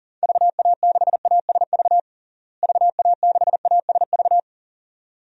Morse code, heard as a keyed tone, about 40 wpm, 700 Hz. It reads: VA6ASV VA6ASV
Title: DXCC-100 Call Signs - 6 Characters - Repeated 2x - 40wpm